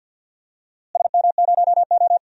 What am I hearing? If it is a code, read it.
Morse code, heard as a keyed tone, about 49 wpm, 700 Hz. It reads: SM0O
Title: DXCC-100 Call Signs - 4 Characters - Repeated 2x - 50wpm